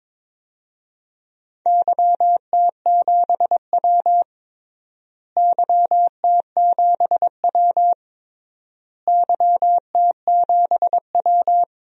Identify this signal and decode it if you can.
Morse code, heard as a keyed tone, about 22 wpm, 700 Hz. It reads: YT7W YT7W YT7W
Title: DXCC-100 Call Signs - 4 Characters - Repeated 3x - 22wpm